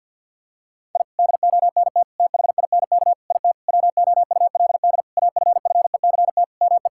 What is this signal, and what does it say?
Morse code, heard as a keyed tone, about 51 wpm, 700 Hz. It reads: I DONT THINK IT WOULD AFFECT ME
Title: Sentences from Top 1000 Words - Review Part 6 - 50wpm